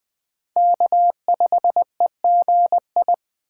Morse code, heard as a keyed tone, about 20 wpm, 700 Hz. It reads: K5EGI
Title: USA Call Signs - 5 Characters - 3x Character Spacing - 20wpm